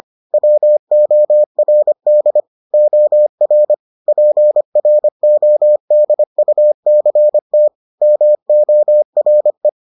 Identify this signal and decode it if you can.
Morse code, heard as a keyed tone, about 25 wpm, 600 Hz. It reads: WORD OR PRODUCT MORE